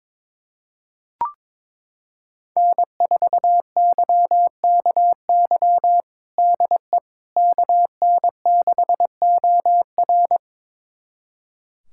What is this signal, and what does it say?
Morse code, heard as a keyed tone, about 22 wpm, 700 Hz. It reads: N4YKY DE KN6OR
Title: USA Callsign DE Callsign 22wpm